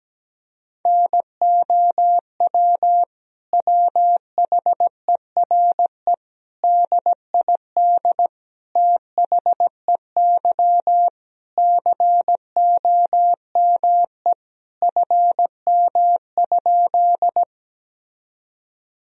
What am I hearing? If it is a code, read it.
Morse code, heard as a keyed tone, about 17 wpm, 700 Hz. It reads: NOW WHERE DID THEY COME FM?
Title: Sentences from Top 200 Words and Abbreviations 17wpm